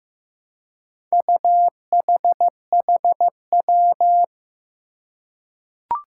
Morse code, heard as a keyed tone, about 15 wpm, 700 Hz. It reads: UHHW
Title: Four Challenging Letter-Number Combinations 15wpm